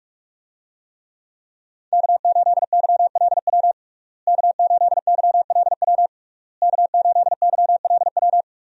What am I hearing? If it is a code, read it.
Morse code, heard as a keyed tone, about 41 wpm, 700 Hz. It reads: K8YLW K8YLW K8YLW
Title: USA Call Signs - 5 Characters - Repeated 3x - 45wpm